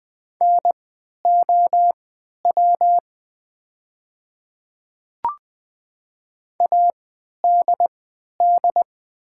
Morse code, heard as a keyed tone, about 20 wpm, spaced as for 11 wpm, 690 Hz. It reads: NOW ADD
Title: 3x Character Spacing - Top 200 Words - 20wpm